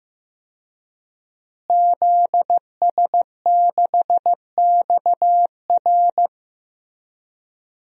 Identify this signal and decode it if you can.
Morse code, heard as a keyed tone, about 15 wpm, 700 Hz. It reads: ZS6XR
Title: DXCC-100 Call Signs - 5 Characters - Repeated 3x - 15wpm